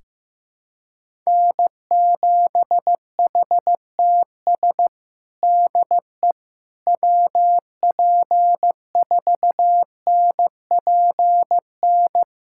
Morse code, heard as a keyed tone, about 15 wpm, 700 Hz. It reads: N7HTS DE WP4NPN
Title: USA Callsign DE Callsign 15wpm